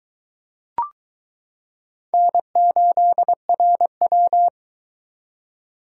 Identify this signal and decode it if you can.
Morse code, heard as a keyed tone, about 22 wpm, 700 Hz. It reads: N8RW